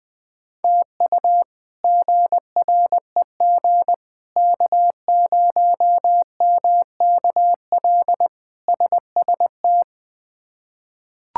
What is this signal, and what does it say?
Morse code, heard as a keyed tone, about 20 wpm, 700 Hz. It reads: TU GREG K0MKL SST
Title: K1USN SST QSO Practice - 20wpm